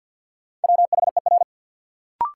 Morse code, heard as a keyed tone, about 47 wpm, 700 Hz. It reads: WHER
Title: Sentences from Top 1500 Words - New Word Drill Alt - 50wpm